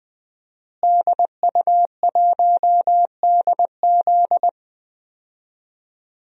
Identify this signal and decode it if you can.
Morse code, heard as a keyed tone, about 20 wpm, 700 Hz. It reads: DU1DZ
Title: DXCC-100 Call Signs - 5 Characters - Repeated 3x - 20wpm